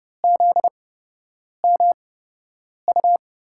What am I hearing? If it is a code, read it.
Morse code, heard as a keyed tone, about 30 wpm, spaced as for 8 wpm, 700 Hz. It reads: ZMU
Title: USA Call Signs - 8x Character Spacing - 30wpm